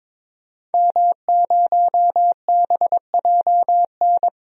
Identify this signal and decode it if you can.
Morse code, heard as a keyed tone, about 22 wpm, 700 Hz. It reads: M0BJN
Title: DXCC-100 Call Signs - 5 Characters - Repeated 3x - 22wpm